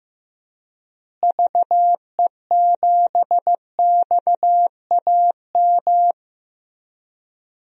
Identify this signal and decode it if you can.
Morse code, heard as a keyed tone, about 15 wpm, 700 Hz. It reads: VE7XAM
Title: DXCC-100 Call Signs - 6 Characters - Repeated 3x - 15wpm